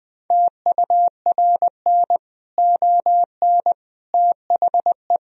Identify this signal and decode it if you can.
Morse code, heard as a keyed tone, about 20 wpm, 700 Hz. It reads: TURN ON THE
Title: Sentences from Top 1000 Words - Review Part 3 - 20wpm